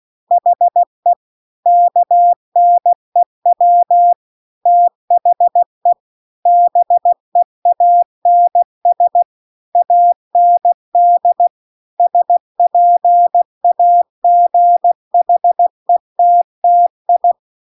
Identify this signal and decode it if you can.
Morse code, heard as a keyed tone, about 16 wpm, 700 Hz. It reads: HE KNEW THE BEANS AND SPAGHETTI